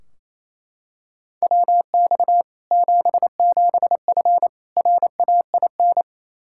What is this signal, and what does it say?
Morse code, heard as a keyed tone, about 28 wpm, 700 Hz. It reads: WX 77F RAIN
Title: Practice QSO - Exchange 2 - Info - 28wpm